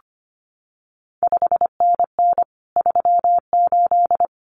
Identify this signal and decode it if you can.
Morse code, heard as a keyed tone, about 25 wpm, 700 Hz. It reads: 5NN 38
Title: CQ WW DX Contest Exchange Encore - 25wpm